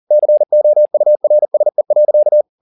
Morse code, heard as a keyed tone, about 40 wpm, 600 Hz. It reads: COURSE.
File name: the old man and the sea 2 40wpm